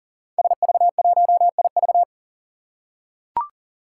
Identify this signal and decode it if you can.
Morse code, heard as a keyed tone, about 38 wpm, 700 Hz. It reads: SV1IV
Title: Five Challenging Letter-Number Combinations 40wpm